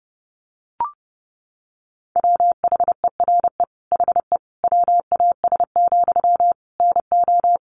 Morse code, heard as a keyed tone, about 30 wpm, 700 Hz. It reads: WHERE HE WAS, NO
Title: Sentences from Top 400 Words 30wpm